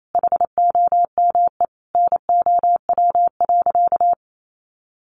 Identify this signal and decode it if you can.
Morse code, heard as a keyed tone, about 28 wpm, 700 Hz. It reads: HOME NOW.
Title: Sentences from Top 1300 Words 28wpm